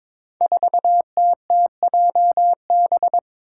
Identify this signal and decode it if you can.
Morse code, heard as a keyed tone, about 22 wpm, 700 Hz. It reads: 4TTJB
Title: Five Challenging Letter-Number Combinations 22wpm